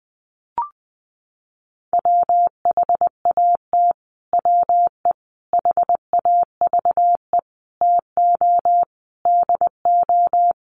Morse code, heard as a keyed tone, about 20 wpm, 700 Hz. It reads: WHAT WE HAVE TO DO
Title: Sets of 5 Words Encore 20wpm